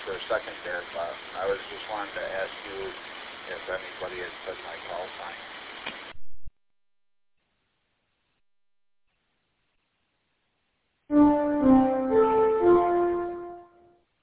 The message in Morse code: none

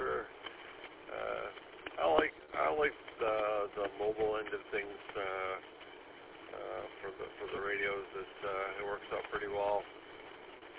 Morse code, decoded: none

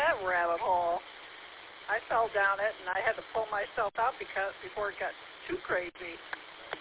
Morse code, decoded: none

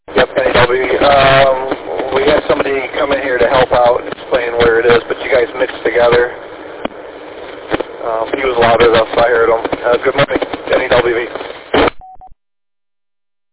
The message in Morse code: none

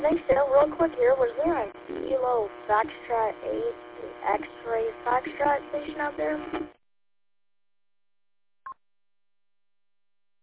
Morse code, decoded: none

